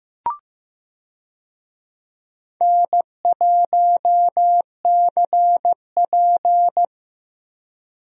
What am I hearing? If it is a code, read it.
Morse code, heard as a keyed tone, about 16 wpm, 700 Hz. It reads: N1CP
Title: USA Call Signs - 4 Characters - Repeated 2x - 15wpm